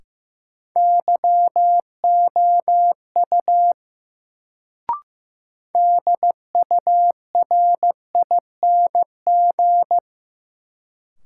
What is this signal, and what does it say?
Morse code, heard as a keyed tone, about 15 wpm, 700 Hz. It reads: YOU DURING
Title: Sets of 2 Words - New Word Drill - 15wpm